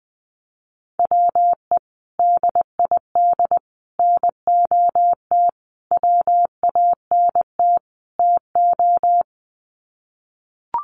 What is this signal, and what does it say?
Morse code, heard as a keyed tone, about 20 wpm, 700 Hz. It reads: WE DID NOT WANT TO
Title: Sets of 5 Words Encore 20wpm